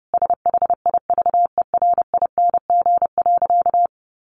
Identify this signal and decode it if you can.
Morse code, heard as a keyed tone, about 30 wpm, 700 Hz. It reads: SHIVERING.
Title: A Sound Of Thunder 30wpm